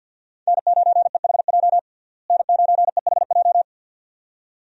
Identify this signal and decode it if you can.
Morse code, heard as a keyed tone, about 52 wpm, 700 Hz. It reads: N9EHJ N9EHJ
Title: USA Call Signs - 5 Characters - Repeated 2x - 50wpm